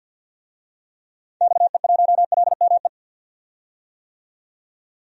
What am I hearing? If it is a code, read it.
Morse code, heard as a keyed tone, about 51 wpm, 700 Hz. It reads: XE1LME
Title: DXCC-100 Call Signs - 6 Characters - 50wpm